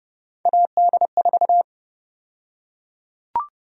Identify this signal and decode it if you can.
Morse code, heard as a keyed tone, about 30 wpm, 700 Hz. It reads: AD4
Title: USA Call Sign Prefixes - Repeated 2x - 30wpm